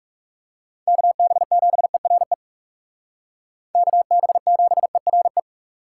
Morse code, heard as a keyed tone, about 36 wpm, 700 Hz. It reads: KB7ERE KB7ERE
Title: Speed Racing - USA 2x3 Call Signs 30wpm